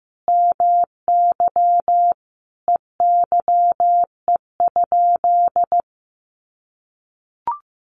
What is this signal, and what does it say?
Morse code, heard as a keyed tone, about 15 wpm, 700 Hz. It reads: MY EYE?
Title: Sentences from Top 500 Words 15wpm